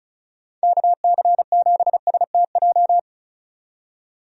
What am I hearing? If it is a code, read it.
Morse code, heard as a keyed tone, about 36 wpm, 700 Hz. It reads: KC7STJ
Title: USA Call Signs - 6 Characters - 35wpm